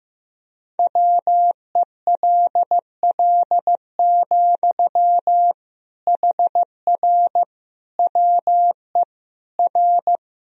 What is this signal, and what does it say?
Morse code, heard as a keyed tone, about 15 wpm, 700 Hz. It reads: WELL, HR WE R
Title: Sentences from Top 300 Words and Abbreviations 15wpm